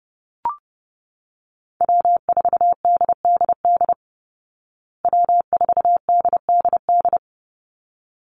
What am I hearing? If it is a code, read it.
Morse code, heard as a keyed tone, about 30 wpm, 700 Hz. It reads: W4DDD W4DDD
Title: Call Signs Repeated 3x - 30wpm